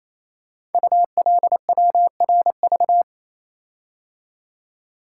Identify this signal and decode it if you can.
Morse code, heard as a keyed tone, about 28 wpm, 700 Hz. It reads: ULWRV
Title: Code Groups - 5 Letters - 28wpm